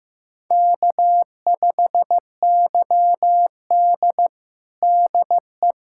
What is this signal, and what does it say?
Morse code, heard as a keyed tone, about 15 wpm, 700 Hz. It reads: K5YD DE
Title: USA Callsign DE Callsign - 4 Characters - 15wpm